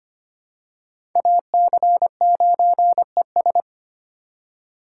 Morse code, heard as a keyed tone, about 25 wpm, 700 Hz. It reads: AC9ES
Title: USA Call Signs - 5 Characters - 4x Character Spacing - 25wpm